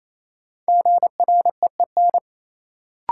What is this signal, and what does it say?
Morse code, heard as a keyed tone, about 28 wpm, 700 Hz. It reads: GREEN
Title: Sentences from Top 500 Words - New Word Drill - 28wpm